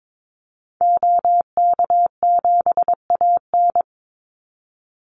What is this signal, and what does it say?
Morse code, heard as a keyed tone, about 22 wpm, 700 Hz. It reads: OK7AN
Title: Call Signs Repeated 3x - 22wpm